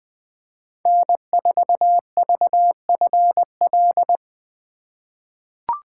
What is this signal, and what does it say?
Morse code, heard as a keyed tone, about 20 wpm, 700 Hz. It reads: N4VFL
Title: USA Call Signs - 3x Character Spacing - 20wpm